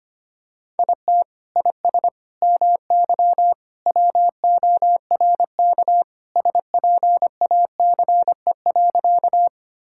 Morse code, heard as a keyed tone, about 25 wpm, 700 Hz. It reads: IT IS MY WORK SPACE.
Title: Sentences from Top 1000 Words - Review Part 5 - 25wpm